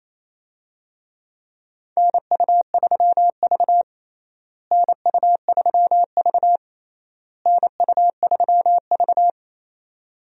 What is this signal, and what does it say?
Morse code, heard as a keyed tone, about 28 wpm, 700 Hz. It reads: NU3V NU3V NU3V
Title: USA Call Signs - 4 Characters - Repeated 3x - 28wpm